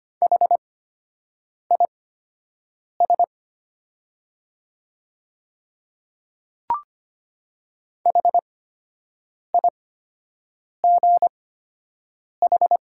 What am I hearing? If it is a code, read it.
Morse code, heard as a keyed tone, about 25 wpm, spaced as for 7 wpm, 700 Hz. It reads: HIS HIGH
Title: 8x Character Spacing - Top 200 Words - 25wpm